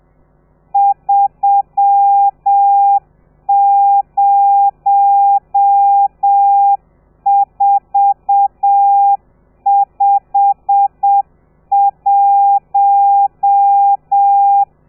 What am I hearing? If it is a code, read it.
Morse code, heard as a keyed tone, about 7 wpm, 800 Hz. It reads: 30451
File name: telegrafia / i2tfj / Lez12 30